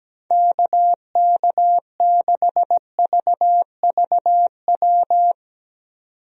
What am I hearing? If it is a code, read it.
Morse code, heard as a keyed tone, about 17 wpm, 700 Hz. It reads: KK6VVW
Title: USA Call Signs - 6 Characters - 17wpm